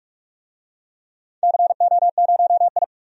Morse code, heard as a keyed tone, about 46 wpm, 700 Hz. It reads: CO0I
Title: DXCC-100 Call Signs - 4 Characters - 45wpm